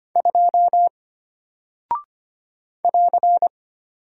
Morse code, heard as a keyed tone, about 25 wpm, 700 Hz. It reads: 2 <AR>
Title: Single Character (Letter:Number:Punctuation:Prosign) Alt 25wpm